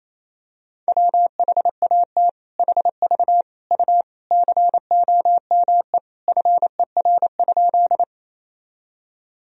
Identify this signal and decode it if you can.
Morse code, heard as a keyed tone, about 28 wpm, 700 Hz. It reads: WHAT HV U COME FER?
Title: Sentences from Top 100 Words and Abbreviations 28wpm